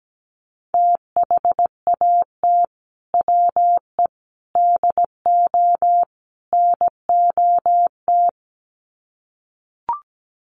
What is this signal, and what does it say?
Morse code, heard as a keyed tone, about 17 wpm, 700 Hz. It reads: THAT WE DO NOT
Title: Sets of 4 Words 17wpm